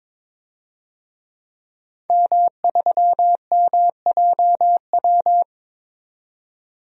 Morse code, heard as a keyed tone, about 22 wpm, 700 Hz. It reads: M3MJW
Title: DXCC-100 Call Signs - 5 Characters - 22wpm